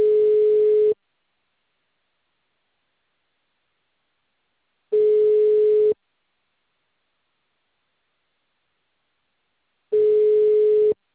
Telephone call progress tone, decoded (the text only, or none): ringback tone